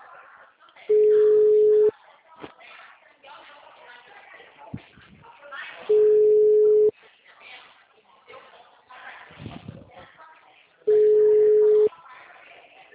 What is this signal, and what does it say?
Ringback tone (Europe, 425 Hz), with speech.
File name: out- -2025- - - .83850